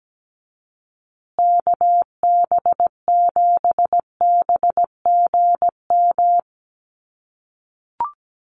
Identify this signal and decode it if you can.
Morse code, heard as a keyed tone, about 17 wpm, 700 Hz. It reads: KB7BGM
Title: USA Call Signs Repeated 3x 17wpm